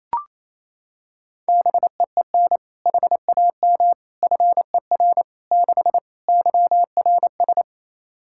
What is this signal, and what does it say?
Morse code, heard as a keyed tone, about 28 wpm, 700 Hz. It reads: BEEN HAM FER 6 YRS
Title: Practice QSO - Exchange 3 - More Info - 28wpm